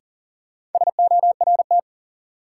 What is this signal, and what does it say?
Morse code, heard as a keyed tone, about 40 wpm, 700 Hz. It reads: SORT
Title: Sets of 2 Words - New Word Drill - 40wpm